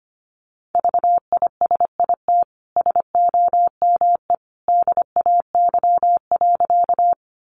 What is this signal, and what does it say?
Morse code, heard as a keyed tone, about 25 wpm, 700 Hz. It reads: VISIT SOME DAY.
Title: Sentences from Top 900 Words 25wpm